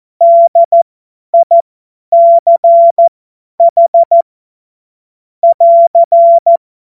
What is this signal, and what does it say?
Morse code, heard as a keyed tone, about 14 wpm, spaced as for 10 wpm, 680 Hz. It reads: DICH <AR>
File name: Morsekurs / Witze / Witz 10. Januar 2023 50 F70